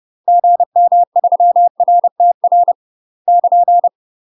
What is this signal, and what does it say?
Morse code, heard as a keyed tone, about 30 wpm, 700 Hz. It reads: GM3RTR <KN>
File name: UK MASTER QSOs TRAINING / 30WPM / UK MASTER QSOs 0035